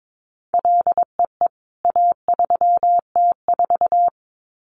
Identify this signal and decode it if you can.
Morse code, heard as a keyed tone, about 22 wpm, 700 Hz. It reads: LEE A3T4
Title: CWT Exchange 22wpm